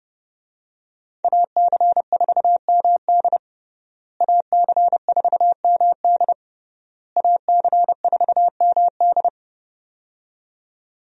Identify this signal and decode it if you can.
Morse code, heard as a keyed tone, about 30 wpm, 700 Hz. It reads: AC4MD AC4MD AC4MD
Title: USA Call Signs - 5 Characters - Repeated 3x - 30wpm